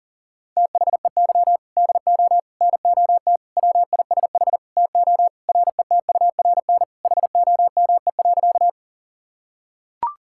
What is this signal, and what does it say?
Morse code, heard as a keyed tone, about 40 wpm, 700 Hz. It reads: THEY DO NOT WISH TO RETURN HOME.
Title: Sentences from Top 1600 Words 40wpm